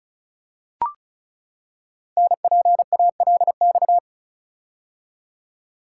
Morse code, heard as a keyed tone, about 36 wpm, 700 Hz. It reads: NPALX
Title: Code Groups - 5 Letter:Numbers - 35wpm